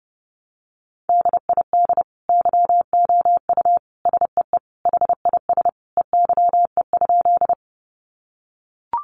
Morse code, heard as a keyed tone, about 30 wpm, 700 Hz. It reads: DID YOU SEE HIS EYE?
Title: Sentences from Top 300 Words 30wpm